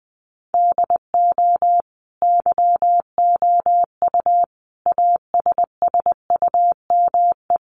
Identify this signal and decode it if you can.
Morse code, heard as a keyed tone, about 20 wpm, 700 Hz. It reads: DO YOU ASSUME
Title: Sentences from Top 1500 Words 20wpm